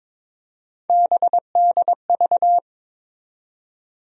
Morse code, heard as a keyed tone, about 22 wpm, 700 Hz. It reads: BDV
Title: USA Call Sign Postfixes - 3 Letters 22wpm